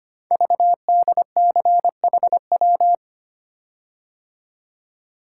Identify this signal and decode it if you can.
Morse code, heard as a keyed tone, about 25 wpm, 700 Hz. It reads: VDCHW
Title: Code Groups - 5 Letters - 25wpm